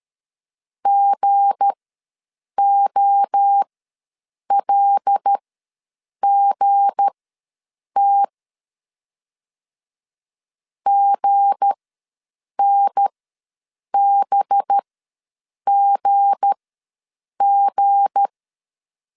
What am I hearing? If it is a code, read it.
Morse code, heard as a keyed tone, about 13 wpm, spaced as for 7 wpm, 800 Hz. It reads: GOLGT GNBGG
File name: LEZIONI / 7 GF9Z / 7 G ws 12wpm 3140